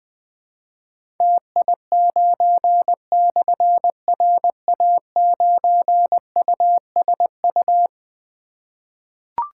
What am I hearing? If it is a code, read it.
Morse code, heard as a keyed tone, about 21 wpm, 700 Hz. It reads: TI9/RA9USU
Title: Difficult Call Signs 20wpm